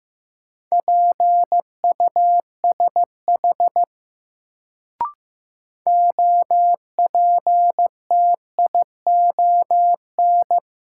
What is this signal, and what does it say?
Morse code, heard as a keyed tone, about 15 wpm, 700 Hz. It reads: PUSH OPTION
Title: Sentences from Top 1300 Words - New Word Drill - 15wpm